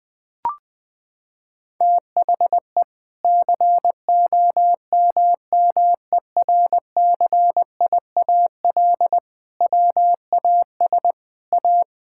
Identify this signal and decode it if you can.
Morse code, heard as a keyed tone, about 20 wpm, 700 Hz. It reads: THE COMMERCIAL WAS A
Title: Sentences from Top 1000 Words - Review Part 5 - 20wpm